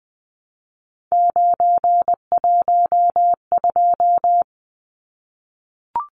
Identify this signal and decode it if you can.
Morse code, heard as a keyed tone, about 20 wpm, 700 Hz. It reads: 912
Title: Numbers - 3-Digit 20wpm